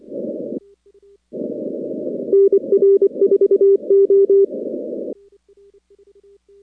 Morse code, heard as a keyed tone, about 24 wpm, 390 Hz. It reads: NR4O
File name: contests / 2019-cw-wpx-kt5j-k5pi- -2330 0-41